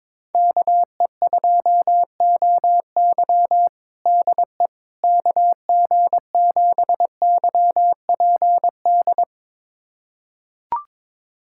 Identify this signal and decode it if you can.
Morse code, heard as a keyed tone, about 22 wpm, 700 Hz. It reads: KE2OY DE KG7YPD
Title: USA Callsign DE Callsign 22wpm